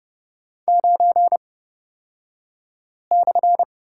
Morse code, heard as a keyed tone, about 30 wpm, spaced as for 5 wpm, 700 Hz. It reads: 9/
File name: Lesson 37d - Letter x - Two Characters 30wpm